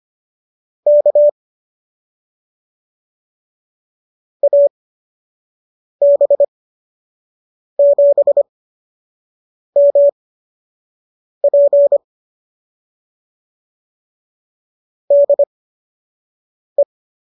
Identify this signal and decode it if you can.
Morse code, heard as a keyed tone, about 25 wpm, spaced as for 6 wpm, 570 Hz. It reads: K AB7MP DE